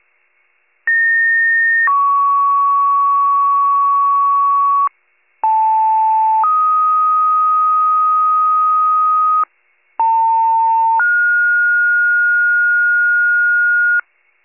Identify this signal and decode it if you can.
Two-tone page, heard as a one-second tone, then a three-second tone, about 1850 then 1120 Hz, about 880 then 1250 Hz, about 900 then 1430 Hz. It is heard from 0.9 to 14.0 s.